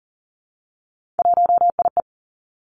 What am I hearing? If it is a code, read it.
Morse code, heard as a keyed tone, about 41 wpm, 700 Hz. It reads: 1IE
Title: Three Challenging Letter-Number Combinations 40wpm